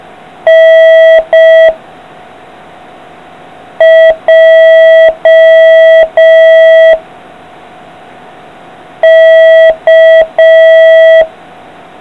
Morse code, heard as a keyed tone, about 5 wpm, 660 Hz. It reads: N J K